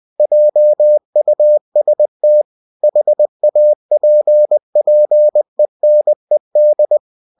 Morse code, heard as a keyed tone, about 20 wpm, 600 Hz. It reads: JUST HAPPENED
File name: cw-books / winnie-the-pooh / winnie-20wpm-page-0005